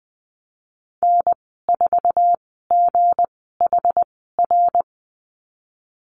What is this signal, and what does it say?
Morse code, heard as a keyed tone, about 19 wpm, spaced as for 14 wpm, 700 Hz. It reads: N4GHR
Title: 207 Lesson 33g - Letter j - Call Signs 20wpm